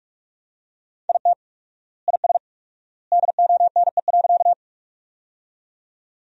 Morse code, heard as a keyed tone, about 47 wpm, 690 Hz. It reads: IT IS DONE.
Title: 4x Word-Spacing - Sentences from Top 500 Words - 45wpm